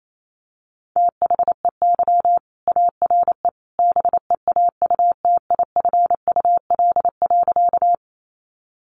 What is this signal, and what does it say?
Morse code, heard as a keyed tone, about 28 wpm, 700 Hz. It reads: THEY ARE BEAUTIFUL.
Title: Sentences from Top 900 Words 28wpm